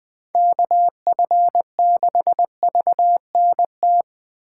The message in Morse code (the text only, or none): KF6VNT